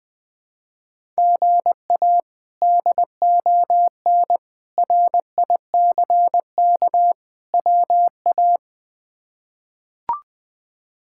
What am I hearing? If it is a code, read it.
Morse code, heard as a keyed tone, about 20 wpm, 700 Hz. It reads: GA DON RICK WA
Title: K1USN SST QSO Practice - 20wpm